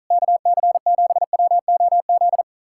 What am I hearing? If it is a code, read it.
Morse code, heard as a keyed tone, about 46 wpm, 700 Hz. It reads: KC7WOZ